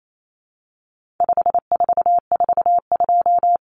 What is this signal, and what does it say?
Morse code, heard as a keyed tone, about 28 wpm, 700 Hz. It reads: 5442
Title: Numbers - 4-Digit 28wpm